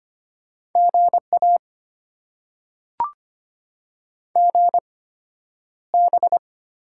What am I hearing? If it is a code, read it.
Morse code, heard as a keyed tone, about 25 wpm, spaced as for 11 wpm, 700 Hz. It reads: GA GB
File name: Top 90 QSO Elements - 8x Character Spacing - 25wpm